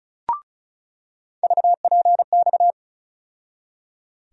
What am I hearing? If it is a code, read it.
Morse code, heard as a keyed tone, about 36 wpm, 700 Hz. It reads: VPX
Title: USA Call Sign Postfixes - 3 Letters 35wpm